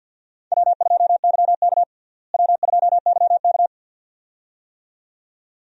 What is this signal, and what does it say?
Morse code, heard as a keyed tone, about 46 wpm, 700 Hz. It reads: W2YK W2YK
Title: USA Call Signs - 4 Characters - Repeated 2x - 50wpm